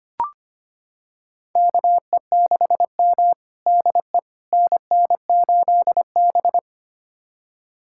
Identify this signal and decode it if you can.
Morse code, heard as a keyed tone, about 25 wpm, 700 Hz. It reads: KE6M DE NN8B
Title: USA Callsign DE Callsign - 4 Characters - 25wpm